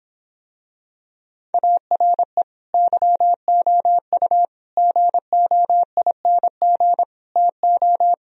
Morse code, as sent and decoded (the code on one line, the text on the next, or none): .- .-. / -.-- --- ..- / --. --- .. -. --. / - ---
ARE YOU GOING TO